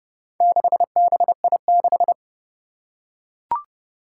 Morse code, heard as a keyed tone, about 30 wpm, 700 Hz. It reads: 6BI6
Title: Four Challenging Letter-Number Combinations 30wpm